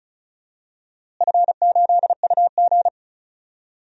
Morse code, heard as a keyed tone, about 36 wpm, 700 Hz. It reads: F8UG
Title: DXCC-100 Call Signs - 4 Characters - Repeated 2x - 35wpm